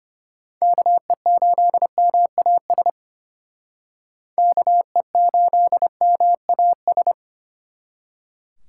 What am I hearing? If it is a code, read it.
Morse code, heard as a keyed tone, about 32 wpm, 700 Hz. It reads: KE8MAS KE8MAS